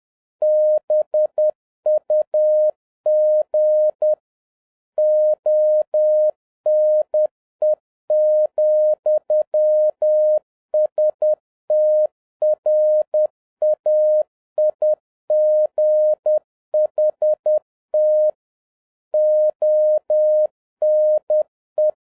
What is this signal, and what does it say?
Morse code, heard as a keyed tone, about 10 wpm, 610 Hz. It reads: BUG ONE,STRAIGHT ONE